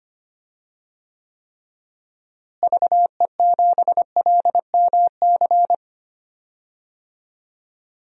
Morse code, heard as a keyed tone, about 25 wpm, 700 Hz. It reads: VE7LMC
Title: DXCC-100 Call Signs - 6 Characters - Repeated 3x - 25wpm